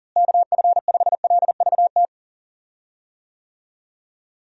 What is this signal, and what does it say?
Morse code, heard as a keyed tone, about 37 wpm, 700 Hz. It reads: KF5LVT
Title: USA Call Signs - 6 Characters - Repeated 3x - 40wpm